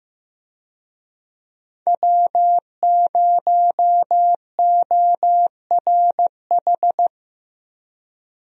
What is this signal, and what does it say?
Morse code, heard as a keyed tone, about 16 wpm, 700 Hz. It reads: W0ORH
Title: USA Call Signs - 5 Characters - Repeated 3x - 15wpm